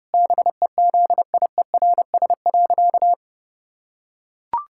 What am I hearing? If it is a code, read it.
Morse code, heard as a keyed tone, about 30 wpm, 700 Hz. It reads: BEZIERS.
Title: Pangram Challenge v2 - 30wpm